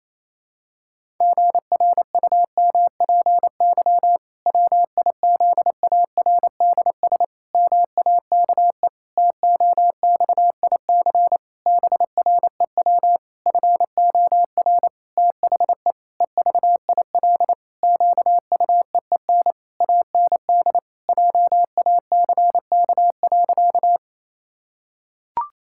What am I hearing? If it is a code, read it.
Morse code, heard as a keyed tone, about 28 wpm, 700 Hz. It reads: GRUMPY WIZARDS MAKE TOXIC BREW FOR THE EVIL QUEEN AND JACK.
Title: Pangram Challenge v2 - 28wpm